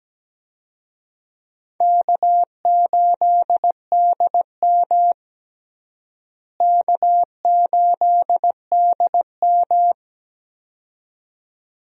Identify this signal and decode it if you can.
Morse code, heard as a keyed tone, about 17 wpm, 700 Hz. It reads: K8DM K8DM
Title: USA Call Signs - 4 Characters - Repeated 2x - 17wpm